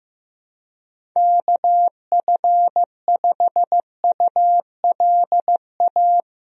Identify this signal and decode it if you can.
Morse code, heard as a keyed tone, about 15 wpm, 700 Hz. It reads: KF5ULA
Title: USA Call Signs - 6 Characters - 15wpm